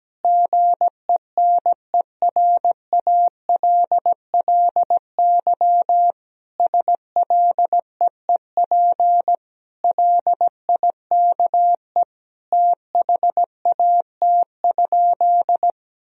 Morse code, heard as a keyed tone, about 17 wpm, 700 Hz. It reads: GENERALLY SLEEP LIKE THAT?
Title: Sentences from Top 1000 Words - Review Part 4 - 17wpm